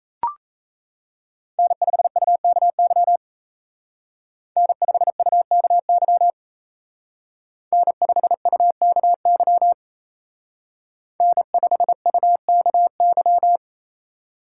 Morse code, heard as a keyed tone, about 36 wpm, 700 Hz. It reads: N5UKY N5UKY N5UKY N5UKY